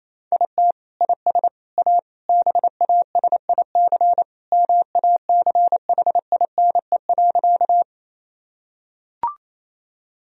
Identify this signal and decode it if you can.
Morse code, heard as a keyed tone, about 28 wpm, 700 Hz. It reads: IT IS A BASIC MACHINE.
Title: Sentences from Top 1000 Words - Review Part 3 - 28wpm